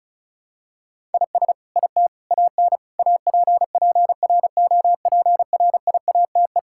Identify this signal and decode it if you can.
Morse code, heard as a keyed tone, about 36 wpm, 700 Hz. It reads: IS IT AN APPROPRIATE
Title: Sentences from Top 1000 Words - Review Part 5 - 35wpm